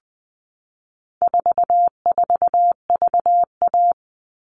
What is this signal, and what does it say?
Morse code, heard as a keyed tone, about 20 wpm, 700 Hz. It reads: 44VA
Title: All Cut Numbers - 4-Digit 20wpm